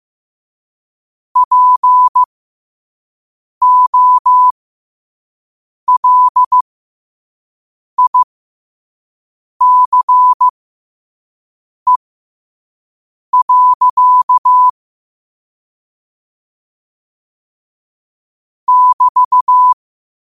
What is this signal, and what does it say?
Morse code, heard as a keyed tone, about 15 wpm, spaced as for 5 wpm, 1000 Hz. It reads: POLICE. <BT>
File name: qotdfiles / 5wpm